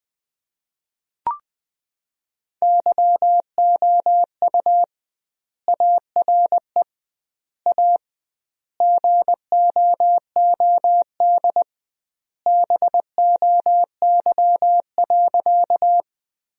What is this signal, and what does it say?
Morse code, heard as a keyed tone, about 21 wpm, 700 Hz. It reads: YOU ARE A GOOD BOY.
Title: 2x Word-Spacing - Sentences from Top 200 Words - 20wpm